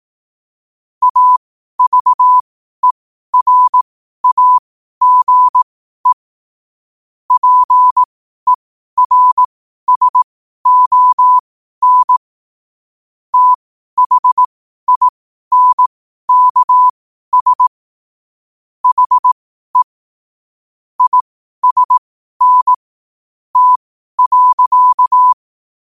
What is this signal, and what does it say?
Morse code, heard as a keyed tone, about 18 wpm, spaced as for 12 wpm, 1000 Hz. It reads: AVERAGE PERSON THINKS HE ISN T.
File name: qotdfiles / 12wpm